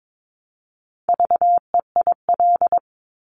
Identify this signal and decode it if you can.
Morse code, heard as a keyed tone, about 22 wpm, 700 Hz. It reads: VEIL
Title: Multi-letter Phonemes Repeated 3x 22wpm